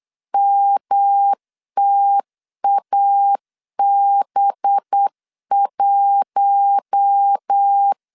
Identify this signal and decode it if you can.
Morse code, heard as a keyed tone, about 8 wpm, 800 Hz. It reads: MTAB1
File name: LEZIONI / 6 PKX0 / 6 PK 8wpm 3003